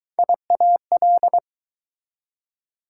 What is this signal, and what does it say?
Morse code, heard as a keyed tone, about 23 wpm, 700 Hz. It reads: IAL